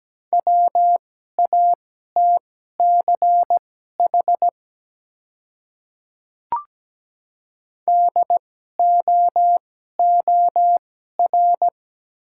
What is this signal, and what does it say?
Morse code, heard as a keyed tone, about 17 wpm, spaced as for 12 wpm, 700 Hz. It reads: WATCH DOOR
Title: 2x Character Spacing - Top 300 Words - 17wpm